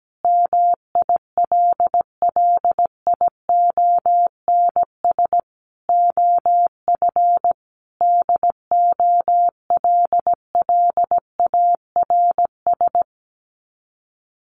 Morse code, heard as a keyed tone, about 17 wpm, 700 Hz. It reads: MILLIONS OF DOLLARS
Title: Sets of 5 Words Encore 17wpm